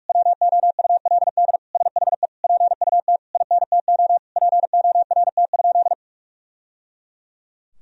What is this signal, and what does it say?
Morse code, heard as a keyed tone, about 46 wpm, 700 Hz. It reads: WOULD SHE PUT INTO PORT?
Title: Sentences from Top 1000 Words - Review Part 1 - 45wpm